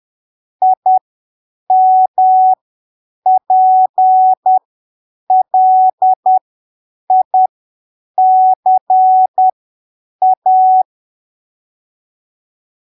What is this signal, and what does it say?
Morse code, heard as a keyed tone, about 10 wpm, spaced as for 7 wpm, 750 Hz. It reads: IMPLICA
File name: arquivos / texto / texto 06 10ppm